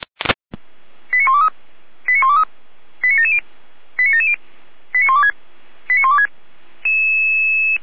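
Five-tone selective call (ZVEI), heard as quick runs of five tones, tones of about 80 ms. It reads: 89123 89123 89770 89770 89126 89126
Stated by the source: ZVEI five-tone selective call received by SDR.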